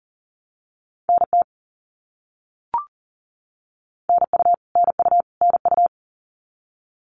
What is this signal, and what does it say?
Morse code, heard as a keyed tone, about 40 wpm, 700 Hz. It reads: NT NU NU NU
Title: Canadian Provinces and Territories - Repeated 3x - 40wpm